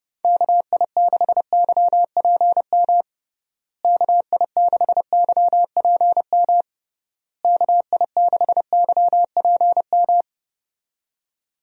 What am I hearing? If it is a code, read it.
Morse code, heard as a keyed tone, about 32 wpm, 700 Hz. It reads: KI6YPM KI6YPM KI6YPM
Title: USA Call Signs - 6 Characters - Repeated 3x - 30wpm